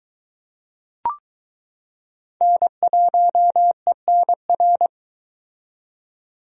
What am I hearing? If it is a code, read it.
Morse code, heard as a keyed tone, about 22 wpm, 700 Hz. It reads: N1ENR